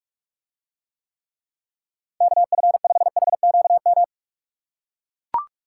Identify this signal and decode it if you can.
Morse code, heard as a keyed tone, about 44 wpm, 700 Hz. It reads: KF5HQM